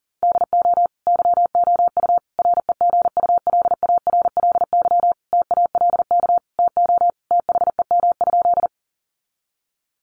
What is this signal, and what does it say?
Morse code, heard as a keyed tone, about 40 wpm, 700 Hz. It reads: DO YOU REGULARLY TALK TO THEM?
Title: Sentences from Top 1400 Words 40wpm